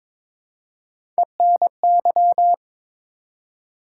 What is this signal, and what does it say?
Morse code, heard as a keyed tone, about 22 wpm, 700 Hz. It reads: ENY
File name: ARRL Sections v2 - 22wpm